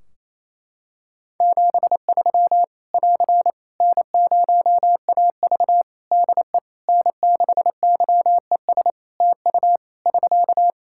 Morse code, heard as a keyed tone, about 28 wpm, 700 Hz. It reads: 73 <AR> N0AV DE N6YES TU <SK>
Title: Practice QSO - Exchange 4 - Ending - 28wpm